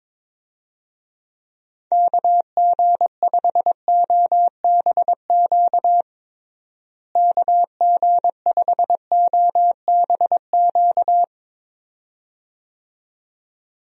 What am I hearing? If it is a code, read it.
Morse code, heard as a keyed tone, about 22 wpm, 700 Hz. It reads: KG5OBQ KG5OBQ
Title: USA Call Signs - 6 Characters - Repeated 2x - 22wpm